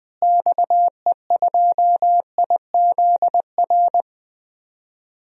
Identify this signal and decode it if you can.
Morse code, heard as a keyed tone, about 20 wpm, 700 Hz. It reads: XE2IZR
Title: DXCC-100 Call Signs - 6 Characters - Repeated 2x - 20wpm